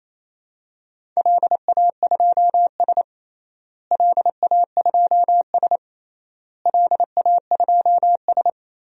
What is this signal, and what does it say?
Morse code, heard as a keyed tone, about 28 wpm, 700 Hz. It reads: LA2S LA2S LA2S
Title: DXCC-100 Call Signs - 4 Characters - Repeated 3x - 28wpm